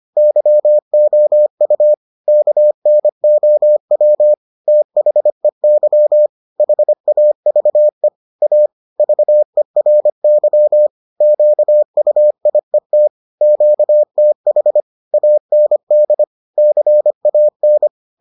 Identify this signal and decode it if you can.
Morse code, heard as a keyed tone, about 25 wpm, 600 Hz. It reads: YOU KNOW THEY HAVE A VERY QUIET QTH AND CAN